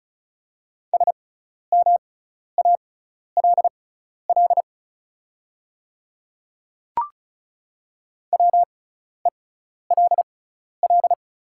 Morse code, heard as a keyed tone, about 36 wpm, spaced as for 12 wpm, 700 Hz. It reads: SMALL WELL
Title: 6x Character Spacing - Top 300 Words - 35wpm